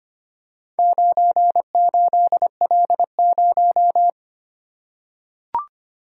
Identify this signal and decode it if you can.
Morse code, heard as a keyed tone, about 25 wpm, 700 Hz. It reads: 98L0
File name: practice / 25wpm / Four Letter-Number Combinations - 25wpm